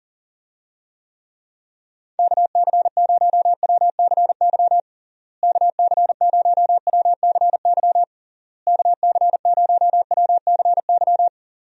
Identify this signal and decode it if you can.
Morse code, heard as a keyed tone, about 40 wpm, 700 Hz. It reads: KC0WCY KC0WCY KC0WCY
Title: USA Call Signs - 6 Characters - Repeated 3x - 40wpm